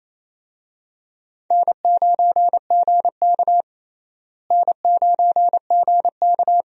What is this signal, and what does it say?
Morse code, heard as a keyed tone, about 27 wpm, 700 Hz. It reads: N9GK N9GK
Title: USA Call Signs - 4 Characters - Repeated 3x - 28wpm